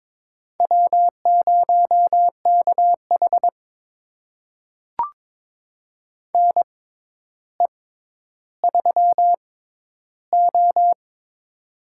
Morse code, heard as a keyed tone, about 21 wpm, spaced as for 11 wpm, 700 Hz. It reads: W0KH NE3O
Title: USA Call Signs - 6x Character Spacing - 22wpm